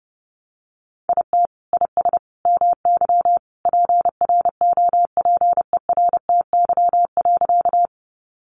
Morse code, heard as a keyed tone, about 30 wpm, 700 Hz. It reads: IT IS MY PROPERTY.
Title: Sentences from Top 1000 Words 30wpm